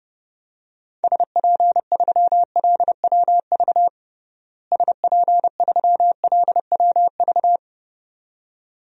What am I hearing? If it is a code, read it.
Morse code, heard as a keyed tone, about 30 wpm, 700 Hz. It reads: SP3LWV SP3LWV
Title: DXCC-100 Call Signs - 6 Characters - Repeated 2x - 30wpm